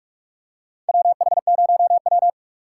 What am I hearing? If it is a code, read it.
Morse code, heard as a keyed tone, about 47 wpm, 700 Hz. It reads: WH0W
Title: USA Call Signs - 4 Characters - 2x Character Spacing - 45wpm